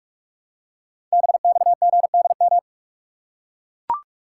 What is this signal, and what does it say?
Morse code, heard as a keyed tone, about 44 wpm, 690 Hz. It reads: BXGDM
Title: Code Groups - 5 Letters - 45wpm